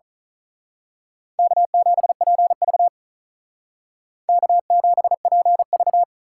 Morse code, heard as a keyed tone, about 39 wpm, 700 Hz. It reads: K7PV K7PV
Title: Speed Racing - USA Call Signs 35wpm